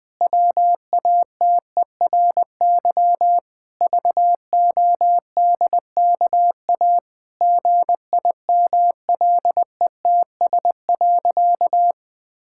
Morse code, heard as a keyed tone, about 20 wpm, 700 Hz. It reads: WATERY VODKA GIMLETS.
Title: Pangram Challenge v2 - 20wpm